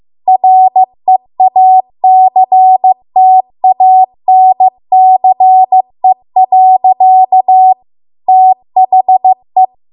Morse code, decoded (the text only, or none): REACTANCE. THE